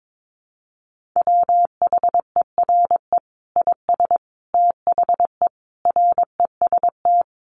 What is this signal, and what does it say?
Morse code, heard as a keyed tone, about 22 wpm, 700 Hz. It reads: WHERE IS THE REST
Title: Sentences from Top 1500 Words 22wpm